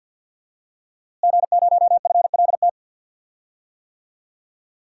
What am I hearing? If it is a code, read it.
Morse code, heard as a keyed tone, about 51 wpm, 700 Hz. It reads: G0FLT
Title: DXCC-100 Call Signs - 5 Characters - Repeated 2x - 50wpm